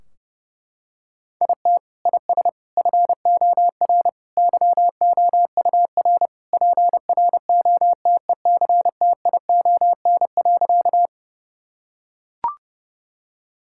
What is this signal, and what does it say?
Morse code, heard as a keyed tone, about 30 wpm, 700 Hz. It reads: IT IS FOR YOUR PROTECTION.
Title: Sentences from Top 1600 Words 30wpm